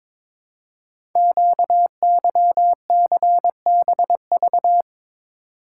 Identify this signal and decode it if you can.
Morse code, heard as a keyed tone, about 22 wpm, 700 Hz. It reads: QYCBV
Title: Code Groups - 5 Letters - 22wpm